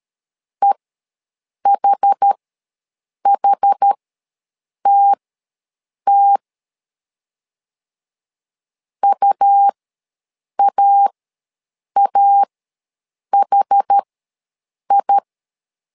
Morse code, heard as a keyed tone, about 13 wpm, spaced as for 7 wpm, 800 Hz. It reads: EHHTT UAAHI
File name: LEZIONI / 2 UHDN / 2 UH ws 12wpm 1730